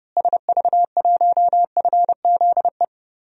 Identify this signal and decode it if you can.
Morse code, heard as a keyed tone, about 30 wpm, 700 Hz. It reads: SV1FZE
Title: DXCC-100 Call Signs - 6 Characters - Repeated 3x - 30wpm